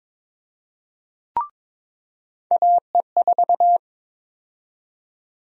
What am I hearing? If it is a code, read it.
Morse code, heard as a keyed tone, about 22 wpm, 700 Hz. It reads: AE4
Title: USA Call Sign Prefixes - 22wpm